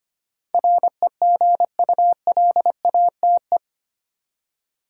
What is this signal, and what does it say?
Morse code, heard as a keyed tone, about 25 wpm, 700 Hz. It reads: REGULATE
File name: 8 Letter Words - First 3 Letters Given - 25wpm